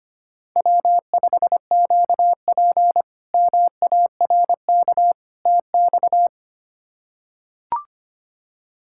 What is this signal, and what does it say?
Morse code, heard as a keyed tone, about 25 wpm, 700 Hz. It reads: W5QP MARK TX
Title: K1USN SST QSO Practice - 25wpm